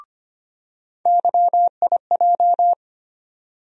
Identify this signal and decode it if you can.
Morse code, heard as a keyed tone, about 25 wpm, 700 Hz. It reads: YIJ
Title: USA Call Sign Postfixes - 3 Letters 25wpm